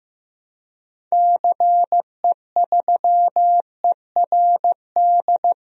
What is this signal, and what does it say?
Morse code, heard as a keyed tone, about 15 wpm, 700 Hz. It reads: CE3ERD
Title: DXCC-100 Call Signs - 6 Characters - Repeated 2x - 15wpm